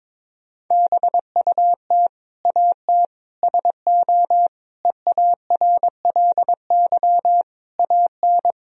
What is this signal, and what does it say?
Morse code, heard as a keyed tone, about 22 wpm, 700 Hz. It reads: BUT AT SO EARLY AN
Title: Sentences from Top 1000 Words - Review Part 3 - 22wpm